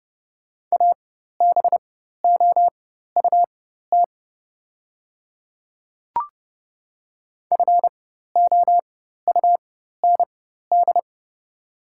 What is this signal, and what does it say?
Morse code, heard as a keyed tone, about 30 wpm, spaced as for 14 wpm, 700 Hz. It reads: ABOUT FOUND
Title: 4x Character Spacing - Top 300 Words - 30wpm